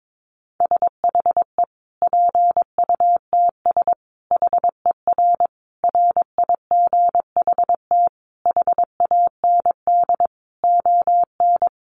Morse code, heard as a keyed tone, about 22 wpm, 700 Hz. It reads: SHE PUTS HER RIGHT HAND ON
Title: Easy Phrases 22wpm